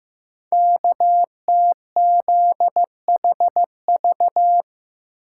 Morse code, heard as a keyed tone, about 15 wpm, 700 Hz. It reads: KTZHV
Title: Code Groups - 5 Letters - 15wpm